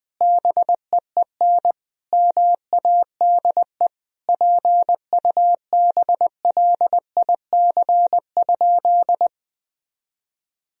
Morse code, heard as a keyed tone, about 20 wpm, 700 Hz. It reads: BEEN MADE PUBLIC?
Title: Sentences from Top 1000 Words - Review Part 3 - 20wpm